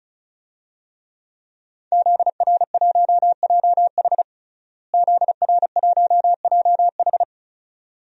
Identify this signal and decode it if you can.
Morse code, heard as a keyed tone, about 36 wpm, 700 Hz. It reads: ZR1JH ZR1JH
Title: DXCC-100 Call Signs - 5 Characters - Repeated 2x - 35wpm